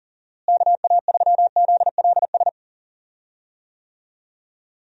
Morse code, heard as a keyed tone, about 40 wpm, 700 Hz. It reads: KA3ZLS
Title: USA Call Signs - 6 Characters - Repeated 3x - 40wpm